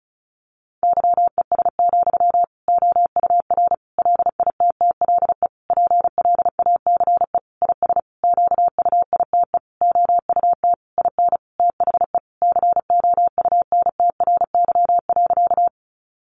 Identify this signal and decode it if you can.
Morse code, heard as a keyed tone, about 34 wpm, 700 Hz. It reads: YES, OUR LITTLE PLACE IS QUITE OUT IN THE COUNTRY.
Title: Sentences from Top 700 Words 35wpm